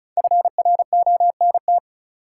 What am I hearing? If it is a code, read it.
Morse code, heard as a keyed tone, about 36 wpm, 700 Hz. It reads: FRONT
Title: Sentences from Top 500 Words - New Word Drill - 35wpm